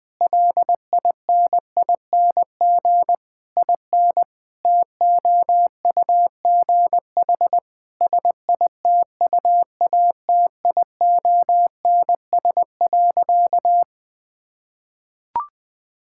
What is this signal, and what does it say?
Morse code, heard as a keyed tone, about 20 wpm, 700 Hz. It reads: LINING IN TOUGH SITUATIONS.
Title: Difficult Sentences with Hints - 20wpm